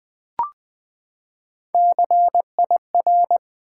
Morse code, heard as a keyed tone, about 20 wpm, 700 Hz. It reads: CIR
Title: Build A Word - 3 Letters At A Time - 9-Letter Words - 20wpm